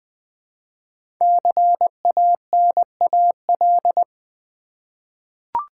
Morse code, heard as a keyed tone, about 20 wpm, 700 Hz. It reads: CANAL
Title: 5 Letter Words - First 3 Letters Given - 20wpm